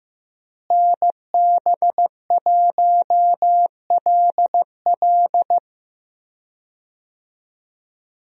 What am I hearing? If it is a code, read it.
Morse code, heard as a keyed tone, about 15 wpm, 700 Hz. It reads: NB1LL
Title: USA Call Signs - 5 Characters - Repeated 3x - 15wpm